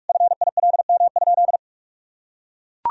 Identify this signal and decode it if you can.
Morse code, heard as a keyed tone, about 41 wpm, 700 Hz. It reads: FILM?
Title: Sentences from Top 1000 Words - Review Part 3 - 45wpm